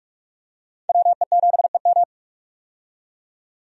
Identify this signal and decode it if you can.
Morse code, heard as a keyed tone, about 45 wpm, 700 Hz. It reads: WE7EM
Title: USA Call Signs - 5 Characters - 45wpm